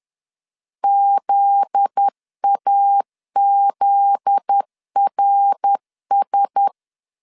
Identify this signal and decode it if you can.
Morse code, heard as a keyed tone, about 10 wpm, 800 Hz. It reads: ZAZRS
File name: LEZIONI / 7 GF9Z / 7 GF9Z 10wpm 2611